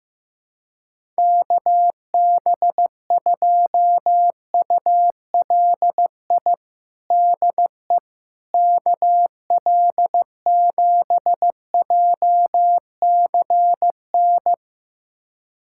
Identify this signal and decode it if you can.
Morse code, heard as a keyed tone, about 15 wpm, 700 Hz. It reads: KB2ULI DE KL7JCN
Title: USA Callsign DE Callsign 15wpm